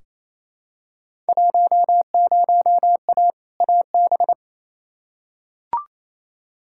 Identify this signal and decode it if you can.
Morse code, heard as a keyed tone, about 28 wpm, 700 Hz. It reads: 10A AB
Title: Field Day Exchanges Encore - 28wpm